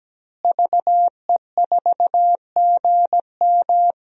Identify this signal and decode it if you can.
Morse code, heard as a keyed tone, about 17 wpm, 700 Hz. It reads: VE4GM
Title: DXCC-100 Call Signs - 5 Characters - Repeated 3x - 17wpm